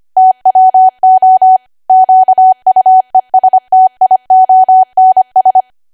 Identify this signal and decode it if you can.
Morse code, heard as a keyed tone, about 25 wpm, 750 Hz. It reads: TWO QUESTIONS
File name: Archive / 25 WPM / 25WPM